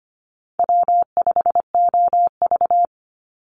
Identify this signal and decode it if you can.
Morse code, heard as a keyed tone, about 25 wpm, 700 Hz. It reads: W5OV
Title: Easy Call Signs Repeated 3x - 25wpm